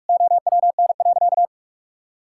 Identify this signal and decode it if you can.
Morse code, heard as a keyed tone, about 42 wpm, 700 Hz. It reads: OWN.
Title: Sentences from Top 1000 Words - Review Part 6 - 45wpm